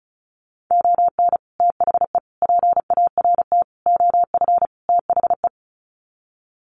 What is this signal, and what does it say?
Morse code, heard as a keyed tone, about 34 wpm, 700 Hz. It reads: ON THE PART OF THE
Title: Sets of 5 Words 35wpm